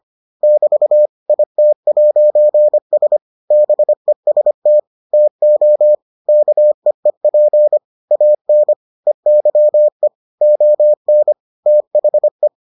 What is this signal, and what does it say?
Morse code, heard as a keyed tone, about 25 wpm, 600 Hz. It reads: <BT> IT'S BEST TO KEEP AN EYE ON THE